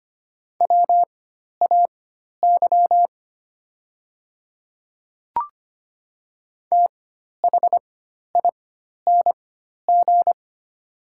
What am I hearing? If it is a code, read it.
Morse code, heard as a keyed tone, about 25 wpm, spaced as for 12 wpm, 700 Hz. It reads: WAY THING
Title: 4x Character Spacing - Top 100 Words - 25wpm